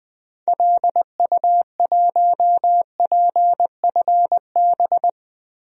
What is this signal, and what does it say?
Morse code, heard as a keyed tone, about 20 wpm, 700 Hz. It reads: LU1PFB
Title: DXCC-100 Call Signs - 6 Characters - Repeated 2x - 20wpm